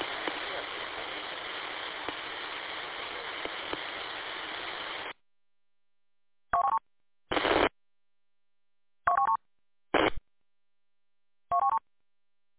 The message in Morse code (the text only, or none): none